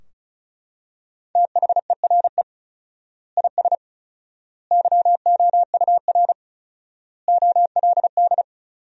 Morse code, heard as a keyed tone, about 36 wpm, 700 Hz. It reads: THERE IS YOUR OLD
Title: 4x Word-Spacing - Sentences from Top 500 Words - 35wpm